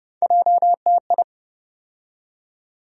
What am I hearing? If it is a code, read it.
Morse code, heard as a keyed tone, about 30 wpm, 700 Hz. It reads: JTI
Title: USA Call Sign Postfixes - 3 Letters 30wpm